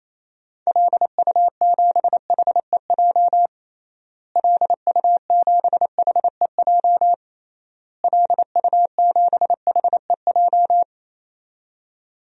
Morse code, heard as a keyed tone, about 28 wpm, 700 Hz. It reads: LU7HEJ LU7HEJ LU7HEJ
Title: DXCC-100 Call Signs - 6 Characters - Repeated 3x - 28wpm